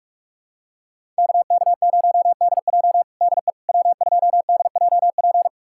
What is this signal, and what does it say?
Morse code, heard as a keyed tone, about 46 wpm, 700 Hz. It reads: KK0DJ DE W2DJP
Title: USA Callsign DE Callsign - 5 Characters - 45wpm